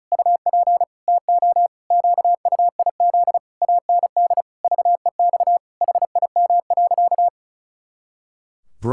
Morse code, heard as a keyed tone, about 34 wpm, 700 Hz. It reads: UP TO QUIZ AND VEX HIM.
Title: Pangram Challenge v2 - 35wpm